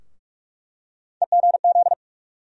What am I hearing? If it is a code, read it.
Morse code, heard as a keyed tone, about 41 wpm, 700 Hz. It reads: EGZ
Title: USA Call Sign Postfixes - 3 Letters 45wpm